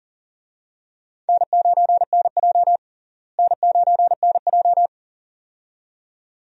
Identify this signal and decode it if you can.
Morse code, heard as a keyed tone, about 40 wpm, 700 Hz. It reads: N9NJ N9NJ
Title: USA Call Signs - 4 Characters - Repeated 2x - 40wpm